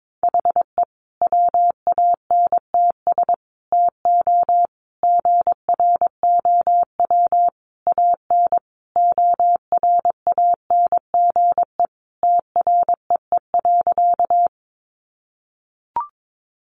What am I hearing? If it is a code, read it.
Morse code, heard as a keyed tone, about 22 wpm, 700 Hz. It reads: HE WANTS TO GROW AN ORANGE TREE.
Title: Easy Phrases 22wpm